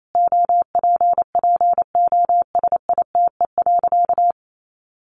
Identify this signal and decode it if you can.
Morse code, heard as a keyed tone, about 28 wpm, 700 Hz. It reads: OPPOSITE.
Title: Sentences from Top 1300 Words 28wpm